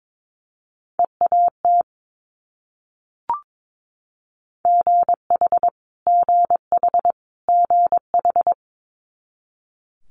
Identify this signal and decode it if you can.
Morse code, heard as a keyed tone, about 22 wpm, 700 Hz. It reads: EAT GH GH GH
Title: Multi-letter Phonemes Repeated 3x 22wpm